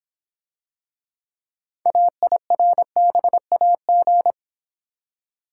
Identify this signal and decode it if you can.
Morse code, heard as a keyed tone, about 26 wpm, 700 Hz. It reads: AIRBAG